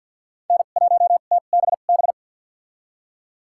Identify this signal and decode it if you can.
Morse code, heard as a keyed tone, about 53 wpm, spaced as for 37 wpm, 700 Hz. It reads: N1TBB
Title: USA Call Signs - 5 Characters - 2x Character Spacing - 50wpm